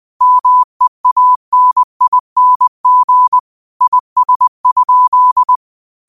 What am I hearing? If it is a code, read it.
Morse code, heard as a keyed tone, about 20 wpm, 1000 Hz. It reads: MEANING IS?
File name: qotdfiles / 20wpm